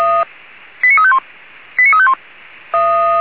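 Five-tone selective call (ZVEI), heard as quick runs of five tones, tones of about 80 ms. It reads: 89351 89351
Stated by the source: Five-tone ZVEI selective call 89349, received by SDR.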